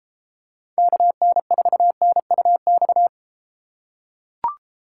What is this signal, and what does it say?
Morse code, heard as a keyed tone, about 32 wpm, 700 Hz. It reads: KN4NUX